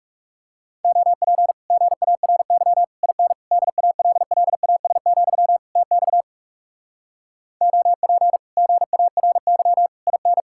Morse code, heard as a keyed tone, about 40 wpm, 700 Hz. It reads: OP GARY IN DALLAS, TX OP GARY IN